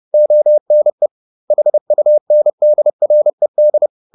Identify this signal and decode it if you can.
Morse code, heard as a keyed tone, about 30 wpm, 600 Hz. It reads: ONE HUNDRED
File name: the old man and the sea / the old man and the sea 1 30wpm